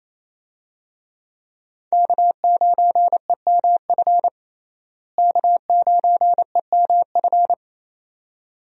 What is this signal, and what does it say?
Morse code, heard as a keyed tone, about 28 wpm, 700 Hz. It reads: K9EMF K9EMF
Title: USA Call Signs - 5 Characters - Repeated 2x - 28wpm